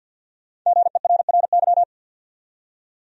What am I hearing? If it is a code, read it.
Morse code, heard as a keyed tone, about 46 wpm, 700 Hz. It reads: GERRY
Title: USA Common Names - Max 5 Letters - 50wpm